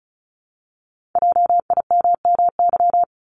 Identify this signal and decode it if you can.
Morse code, heard as a keyed tone, about 36 wpm, 700 Hz. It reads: JIMMY
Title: Multi-letter Phonemes Repeated 3x 35wpm